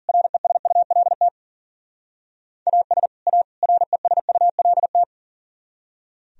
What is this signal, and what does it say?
Morse code, heard as a keyed tone, about 49 wpm, 700 Hz. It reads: RESULT AS A RESULT
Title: Speed-Racing - Sets of 3 Words 40wpm